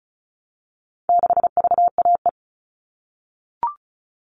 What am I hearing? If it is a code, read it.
Morse code, heard as a keyed tone, about 34 wpm, 700 Hz. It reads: 6VAE
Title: All Cut Numbers - 4-Digit 35wpm